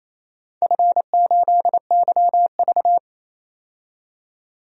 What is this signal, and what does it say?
Morse code, heard as a keyed tone, about 28 wpm, 700 Hz. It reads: F8YV
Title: DXCC-100 Call Signs - 4 Characters - 28wpm